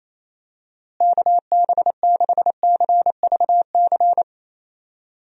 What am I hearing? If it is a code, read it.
Morse code, heard as a keyed tone, about 27 wpm, 700 Hz. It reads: KB6CVC
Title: USA Call Signs - 6 Characters - Repeated 2x - 28wpm